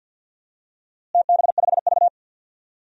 Morse code, heard as a keyed tone, about 51 wpm, 700 Hz. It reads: TB5V
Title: DXCC-100 Call Signs - 4 Characters - Repeated 2x - 50wpm